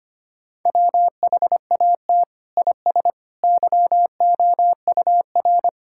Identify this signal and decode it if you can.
Morse code, heard as a keyed tone, about 25 wpm, 700 Hz. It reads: WHAT IS YOUR
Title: Sentences from Top 1000 Words - Review Part 5 - 25wpm